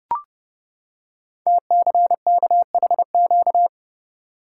Morse code, heard as a keyed tone, about 30 wpm, 700 Hz. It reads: TCKHQ
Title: Code Groups - 5 Letters - 30wpm